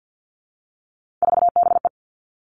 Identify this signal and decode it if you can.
Morse code, heard as a keyed tone, about 50 wpm, 700 Hz. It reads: 4BE
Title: Three Challenging Letter-Number Combinations 50wpm